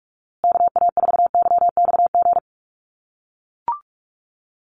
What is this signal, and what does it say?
Morse code, heard as a keyed tone, about 44 wpm, 700 Hz. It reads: KA4YXG
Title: USA Call Signs Repeated 2x 45wpm